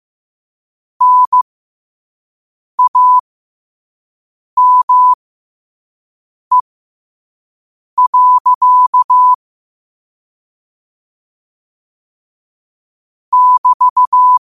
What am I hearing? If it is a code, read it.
Morse code, heard as a keyed tone, about 15 wpm, spaced as for 5 wpm, 1000 Hz. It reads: NAME. <BT>
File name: qotdfiles / 5wpm